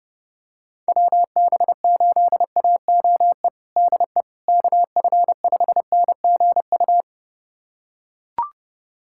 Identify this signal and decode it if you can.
Morse code, heard as a keyed tone, about 30 wpm, 700 Hz. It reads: WB8AOE DE KF5NGU
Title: USA Callsign DE Callsign - 6 Characters - 30wpm